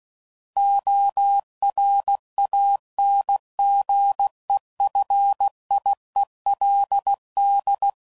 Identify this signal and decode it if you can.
Morse code, heard as a keyed tone, about 16 wpm, 800 Hz. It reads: ORANGEFIELD
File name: CW Code Groups @15WPM / fromSM's / Cities